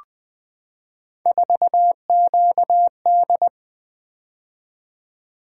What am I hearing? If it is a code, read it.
Morse code, heard as a keyed tone, about 20 wpm, 700 Hz. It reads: 4QD
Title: Three Letter-Number Combinations - 20wpm